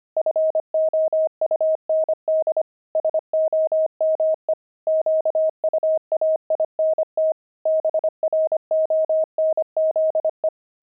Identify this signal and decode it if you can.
Morse code, heard as a keyed tone, about 25 wpm, 630 Hz. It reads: FOUND SOME QUAINT BRONZE
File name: pangram-fs-25-cs-25-deep